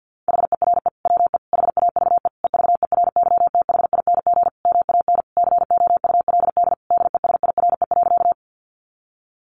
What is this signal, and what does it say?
Morse code, heard as a keyed tone, about 50 wpm, 700 Hz. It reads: HERE WE HAVE EVERYTHING MAN COULD DESIRE.
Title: Sentences from Top 1100 Words 50wpm